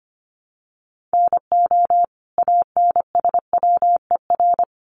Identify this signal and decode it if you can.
Morse code, heard as a keyed tone, about 25 wpm, 700 Hz. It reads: NO ANSWER
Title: Sentences from Top 700 Words 25wpm